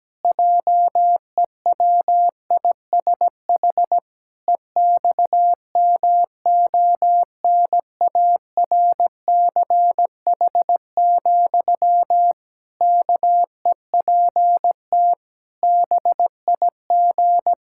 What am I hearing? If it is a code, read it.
Morse code, heard as a keyed tone, about 17 wpm, 700 Hz. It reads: JEWISH EXMONARCH, KEPT BIG
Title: Pangram Challenge v2 - 17wpm